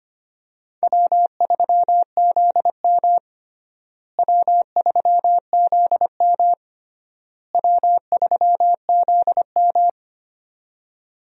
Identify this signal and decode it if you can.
Morse code, heard as a keyed tone, about 25 wpm, 700 Hz. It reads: W3ZM W3ZM W3ZM
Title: USA Call Signs - 4 Characters - Repeated 3x - 25wpm